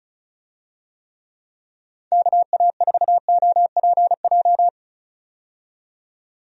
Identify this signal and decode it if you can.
Morse code, heard as a keyed tone, about 34 wpm, 700 Hz. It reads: KA4OPJ
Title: USA Call Signs - 6 Characters - 35wpm